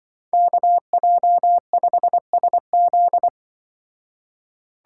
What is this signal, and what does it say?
Morse code, heard as a keyed tone, about 25 wpm, 700 Hz. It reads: KJ5SZ
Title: Speed Racing - USA Call Signs 20wpm